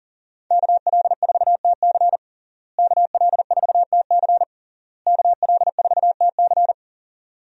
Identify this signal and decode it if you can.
Morse code, heard as a keyed tone, about 40 wpm, 700 Hz. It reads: KL4TC KL4TC KL4TC
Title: USA Call Signs - 5 Characters - Repeated 3x - 40wpm